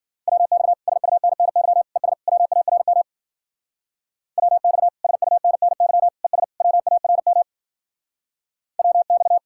Morse code, heard as a keyed tone, about 49 wpm, 700 Hz. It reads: WX SUNNY ES WARM WX SUNNY ES WARM WX
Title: Speed-Racing Common QSO Phrases - 40wpm